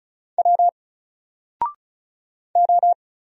Morse code, heard as a keyed tone, about 34 wpm, 700 Hz. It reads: W O
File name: Single-Letter-Number-35wpm